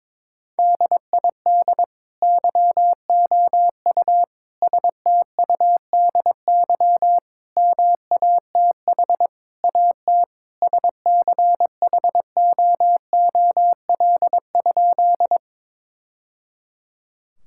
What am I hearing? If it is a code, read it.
Morse code, heard as a keyed tone, about 22 wpm, 700 Hz. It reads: DID YOU STUDY MATH AT SCHOOL?
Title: Sentences from Top 1000 Words - Review Part 6 - 22wpm